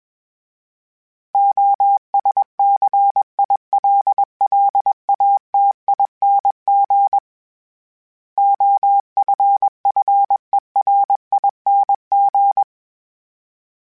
Morse code, heard as a keyed tone, about 21 wpm, 800 Hz. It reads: OSCILLATING OFFERING
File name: Special INGs / Special INGs-mp3s / Sngl Ing @20